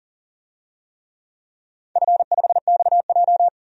Morse code, heard as a keyed tone, about 40 wpm, 700 Hz. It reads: F5XJ
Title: DXCC-100 Call Signs - 4 Characters - 40wpm